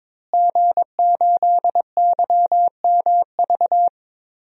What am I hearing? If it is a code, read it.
Morse code, heard as a keyed tone, about 22 wpm, 700 Hz. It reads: G8YMV
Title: DXCC-100 Call Signs - 5 Characters - Repeated 2x - 22wpm